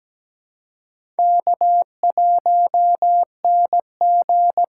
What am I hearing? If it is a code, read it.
Morse code, heard as a keyed tone, about 17 wpm, 700 Hz. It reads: K1NG
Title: USA Call Signs - 4 Characters - 2x Character Spacing - 17wpm